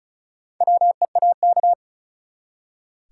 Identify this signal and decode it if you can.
Morse code, heard as a keyed tone, about 34 wpm, 700 Hz. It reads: WEAK
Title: Four Letter Words - 35wpm